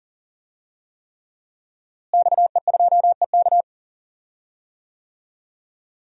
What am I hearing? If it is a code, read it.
Morse code, heard as a keyed tone, about 40 wpm, 700 Hz. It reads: XE2EK
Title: DXCC-100 Call Signs - 5 Characters - Repeated 3x - 40wpm